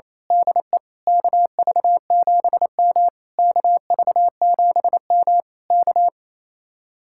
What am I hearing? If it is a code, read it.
Morse code, heard as a keyed tone, about 26 wpm, 700 Hz. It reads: DE KV7M KV7M K
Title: Speed-Racing Common QSO Phrases - 28wpm